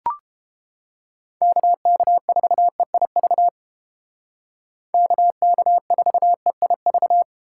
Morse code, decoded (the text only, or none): KK4EIV KK4EIV